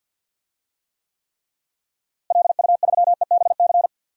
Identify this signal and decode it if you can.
Morse code, heard as a keyed tone, about 51 wpm, 700 Hz. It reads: LU3EBC
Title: DXCC-100 Call Signs - 6 Characters - Repeated 3x - 50wpm